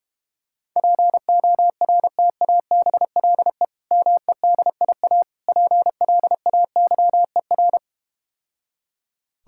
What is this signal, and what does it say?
Morse code, heard as a keyed tone, about 32 wpm, 700 Hz. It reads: PORTABLE MEDIA PLAYER